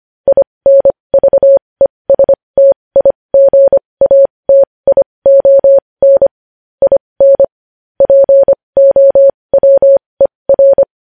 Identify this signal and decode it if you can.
Morse code, heard as a keyed tone, about 25 wpm, spaced as for 20 wpm, 550 Hz. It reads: INVESTIGATION IN POWER
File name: podcast / News headlines-1x-25x20